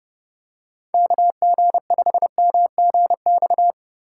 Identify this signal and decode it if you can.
Morse code, heard as a keyed tone, about 32 wpm, 700 Hz. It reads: KG5MGX
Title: USA Call Signs - 6 Characters - 2x Character Spacing - 30wpm